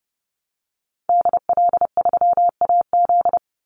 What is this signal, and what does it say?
Morse code, heard as a keyed tone, about 30 wpm, 700 Hz. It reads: DL3AZ
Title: Call Signs Repeated 3x - 30wpm